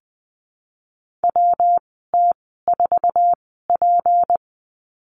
Morse code, heard as a keyed tone, about 20 wpm, spaced as for 14 wpm, 700 Hz. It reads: WT4P
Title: 173 Lesson 28g - Number 9 - Call Signs 20wpm